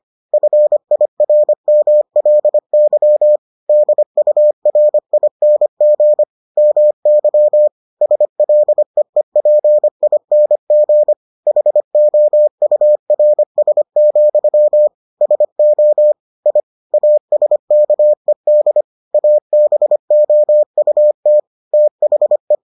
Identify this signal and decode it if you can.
Morse code, heard as a keyed tone, about 25 wpm, 600 Hz. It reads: FIRMLY DURING MY SLEEPING HOURS, SO I ASKED ABOUT THE